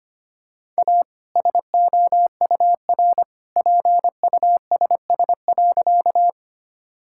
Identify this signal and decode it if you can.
Morse code, heard as a keyed tone, about 25 wpm, 700 Hz. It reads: A SOUR PUSS.
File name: practice / 25wpm / All-Age Appropriate Jokes - 25wpm